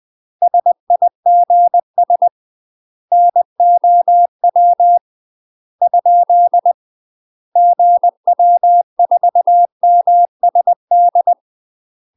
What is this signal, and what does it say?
Morse code, heard as a keyed tone, about 20 wpm, 700 Hz. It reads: SIGS NOW ? GW4MSD
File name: UK MASTER QSOs TRAINING / 20WPM / UK MASTER QSOs 0001